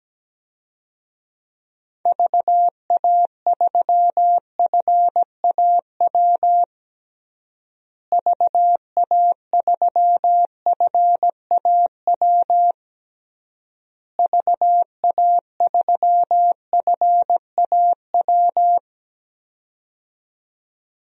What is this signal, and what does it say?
Morse code, heard as a keyed tone, about 17 wpm, 700 Hz. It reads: VA3FAW VA3FAW VA3FAW
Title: DXCC-100 Call Signs - 6 Characters - Repeated 3x - 17wpm